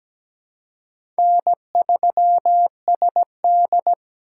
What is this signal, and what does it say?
Morse code, heard as a keyed tone, about 17 wpm, 700 Hz. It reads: N3SD
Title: USA Call Signs - 4 Characters - 3x Character Spacing - 17wpm